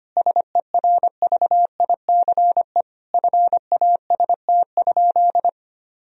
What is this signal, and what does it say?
Morse code, heard as a keyed tone, about 25 wpm, 700 Hz. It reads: SERVICE FAST?
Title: Sentences from Top 1000 Words - Review Part 4 - 25wpm